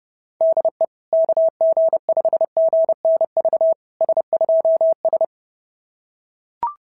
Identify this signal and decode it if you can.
Morse code, heard as a keyed tone, about 30 wpm, 660 Hz. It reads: DE KG5GNV S2S
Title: SOTA QSO Practice 30wpm